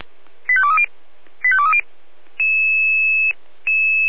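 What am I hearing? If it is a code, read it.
Five-tone selective call (ZVEI), heard as quick runs of five tones, tones of about 80 ms. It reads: 85230 85230
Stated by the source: ZVEI five-tone selective call 85369 received by SDR.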